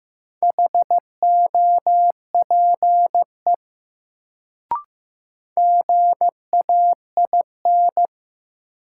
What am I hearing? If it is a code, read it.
Morse code, heard as a keyed tone, about 15 wpm, 700 Hz. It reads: HOPE GAIN
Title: Sentences from Top 1000 Words - New Word Drill - 15wpm